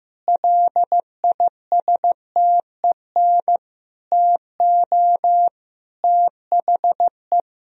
Morse code, heard as a keyed tone, about 15 wpm, 700 Hz. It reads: LISTEN TO THE
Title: Sentences from Top 1000 Words - Review Part 3 - 15wpm